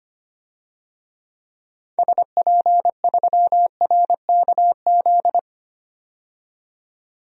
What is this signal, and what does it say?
Morse code, heard as a keyed tone, about 25 wpm, 700 Hz. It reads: SP3RKZ
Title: DXCC-100 Call Signs - 6 Characters - 25wpm